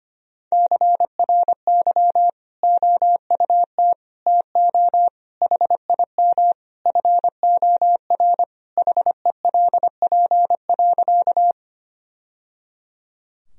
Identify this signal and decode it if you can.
Morse code, heard as a keyed tone, about 25 wpm, 700 Hz. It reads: CRY OUT TO HIM FOR HELP.
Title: Sentences from Top 1000 Words - Review Part 2 - 25wpm